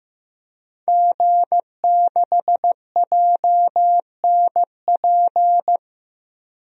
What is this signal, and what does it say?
Morse code, heard as a keyed tone, about 15 wpm, 700 Hz. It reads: G6JNP
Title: DXCC-100 Call Signs - 5 Characters - 15wpm